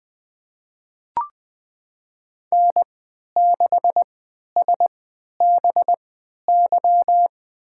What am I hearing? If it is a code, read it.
Morse code, heard as a keyed tone, about 20 wpm, spaced as for 11 wpm, 700 Hz. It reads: N6SBY
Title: USA Call Signs - 5 Characters - 3x Character Spacing - 20wpm